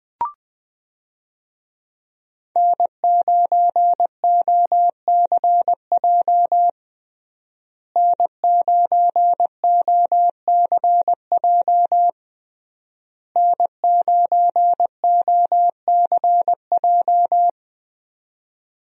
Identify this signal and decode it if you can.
Morse code, heard as a keyed tone, about 20 wpm, 700 Hz. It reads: N9OCJ N9OCJ N9OCJ
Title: USA Call Signs - 5 Characters - Repeated 3x - 20wpm